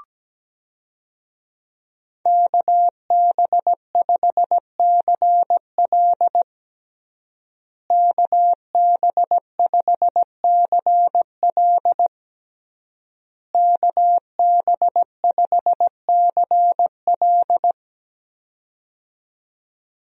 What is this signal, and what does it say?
Morse code, heard as a keyed tone, about 17 wpm, 700 Hz. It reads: KB5CL KB5CL KB5CL
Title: USA Call Signs - 5 Characters - Repeated 3x - 17wpm